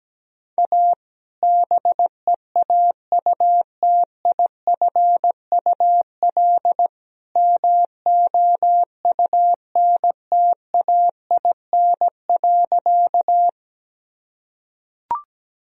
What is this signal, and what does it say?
Morse code, heard as a keyed tone, about 17 wpm, 700 Hz. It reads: A BEAUTIFUL MOUNTAIN.
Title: Sentences from Top 1000 Words - Review Part 1 - 17wpm